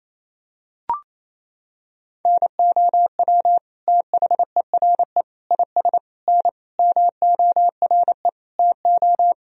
Morse code, heard as a keyed tone, about 28 wpm, 700 Hz. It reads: NOW THERE IS N MORE TO
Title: Sentences from Top 100 Words and Abbreviations 28wpm